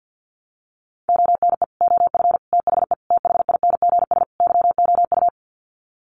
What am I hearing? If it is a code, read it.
Morse code, heard as a keyed tone, about 50 wpm, 700 Hz. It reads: ONE OF THE THINGS YOU
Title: Sets of 5 Words Encore2 50wpm